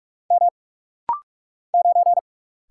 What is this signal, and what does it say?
Morse code, heard as a keyed tone, about 44 wpm, 700 Hz. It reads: M 9
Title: Single Letter-Number - Rapid-Fire 45wpm